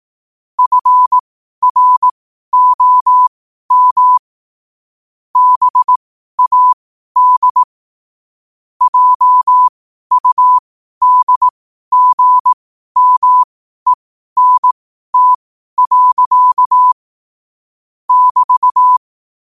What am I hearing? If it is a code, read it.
Morse code, heard as a keyed tone, about 18 wpm, spaced as for 12 wpm, 1000 Hz. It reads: FROM BAD JUDGMENT. <BT>
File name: qotdfiles / 12wpm